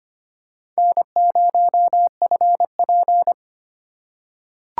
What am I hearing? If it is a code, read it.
Morse code, heard as a keyed tone, about 25 wpm, 700 Hz. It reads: N0FP
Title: USA Call Signs - 4 Characters - 2x Character Spacing - 25wpm